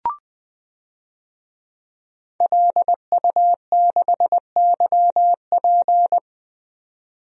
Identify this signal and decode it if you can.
Morse code, heard as a keyed tone, about 20 wpm, 700 Hz. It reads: LU6YP
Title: DXCC-100 Call Signs - 5 Characters - 20wpm